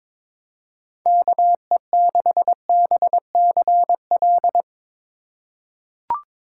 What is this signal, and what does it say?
Morse code, heard as a keyed tone, about 22 wpm, 700 Hz. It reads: KE6BCL
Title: USA Call Signs - 6 Characters - 2x Character Spacing - 22wpm